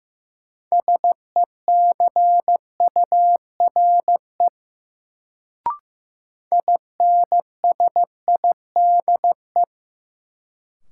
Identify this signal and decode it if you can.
Morse code, heard as a keyed tone, about 15 wpm, 700 Hz. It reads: SECURE INSIDE
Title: Sentences from Top 1200 Words - New Word Drill Alt - 15wpm